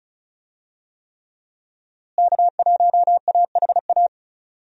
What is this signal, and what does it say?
Morse code, heard as a keyed tone, about 36 wpm, 700 Hz. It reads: K1AHA
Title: USA Call Signs - 5 Characters - Repeated 3x - 35wpm